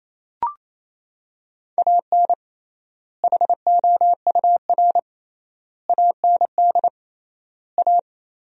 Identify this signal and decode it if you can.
Morse code, heard as a keyed tone, about 28 wpm, 700 Hz. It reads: AN HOUR AND A
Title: 3x Word-Spacing - Sentences from Top 400 Words - 28wpm